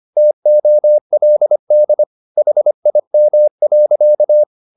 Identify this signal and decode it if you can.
Morse code, heard as a keyed tone, about 25 wpm, 600 Hz. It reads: TOLD HIM.
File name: the old man and the sea part 1 25wpm